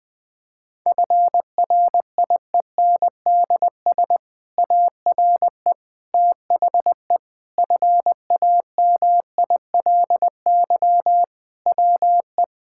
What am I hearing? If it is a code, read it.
Morse code, heard as a keyed tone, about 20 wpm, 700 Hz. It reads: FRIENDS ARE THE FAMILY WE
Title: Difficult Sentences with Hints - 20wpm